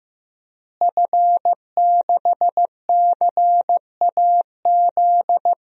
Morse code, heard as a keyed tone, about 15 wpm, 700 Hz. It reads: F6CAZ
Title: DXCC-100 Call Signs - 5 Characters - Repeated 3x - 15wpm